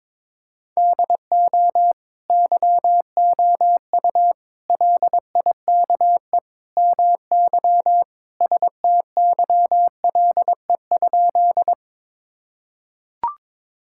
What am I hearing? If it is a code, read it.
Morse code, heard as a keyed tone, about 22 wpm, 700 Hz. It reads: DO YOU LIKE MY STYLE?
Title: Sentences from Top 1000 Words - Review Part 5 - 22wpm